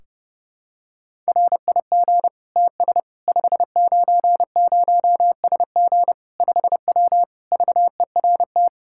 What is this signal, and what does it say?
Morse code, heard as a keyed tone, about 30 wpm, 700 Hz. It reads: RIG TS 590SG 5W VERT
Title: Practice QSO - Exchange 2 - Info - 30wpm